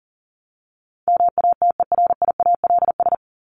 Morse code, heard as a keyed tone, about 37 wpm, 700 Hz. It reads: MATERIALS
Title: ES:S Suffix - Top 100 Words with Base Words 40wpm